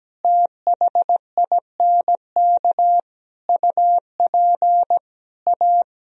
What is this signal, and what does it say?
Morse code, heard as a keyed tone, about 17 wpm, 700 Hz. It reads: THINK UP A
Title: Sentences from Top 1000 Words - Review Part 2 - 17wpm